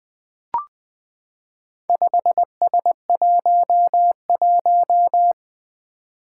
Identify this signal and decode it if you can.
Morse code, heard as a keyed tone, about 20 wpm, 700 Hz. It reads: 5S11
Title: Four Challenging Letter-Number Combinations 20wpm